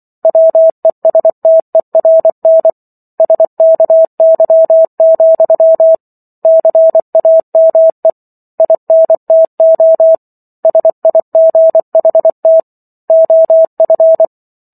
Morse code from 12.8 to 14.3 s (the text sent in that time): OF